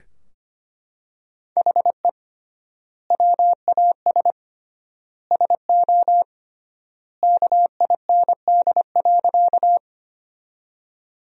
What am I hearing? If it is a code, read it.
Morse code, heard as a keyed tone, about 25 wpm, 700 Hz. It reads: HE WAS SO KIND.
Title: 3x Word-Spacing - Sentences from Top 200 Words - 25wpm